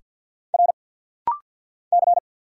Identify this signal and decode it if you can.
Morse code, heard as a keyed tone, about 49 wpm, 700 Hz. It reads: R C
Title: Single Letter-Number Alt - Rapid-Fire 50wpm